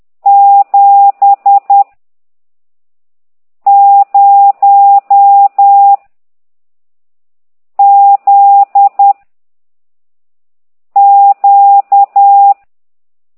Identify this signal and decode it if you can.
Morse code, heard as a keyed tone, about 10 wpm, 800 Hz. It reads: 7 0 Z Q
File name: ct2idw / exerc15